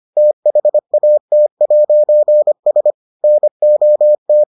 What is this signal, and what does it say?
Morse code, heard as a keyed tone, about 25 wpm, 600 Hz. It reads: THAT'S NOT